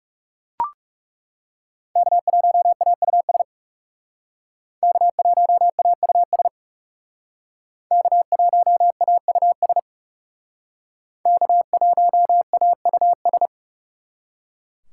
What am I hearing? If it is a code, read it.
Morse code, heard as a keyed tone, about 38 wpm, 700 Hz. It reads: K1AUS K1AUS K1AUS K1AUS